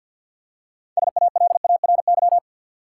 Morse code, heard as a keyed tone, about 52 wpm, 700 Hz. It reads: SALARY
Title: Six Letter Words - 50wpm